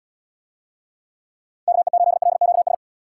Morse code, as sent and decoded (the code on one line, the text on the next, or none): -.- . ---.. .-.. --.- ...
KE8LQS